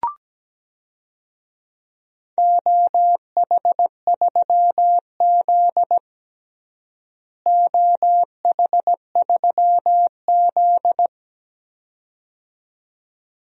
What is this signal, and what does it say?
Morse code, heard as a keyed tone, about 17 wpm, 700 Hz. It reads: OH3Z OH3Z
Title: DXCC-100 Call Signs - 4 Characters - Repeated 2x - 17wpm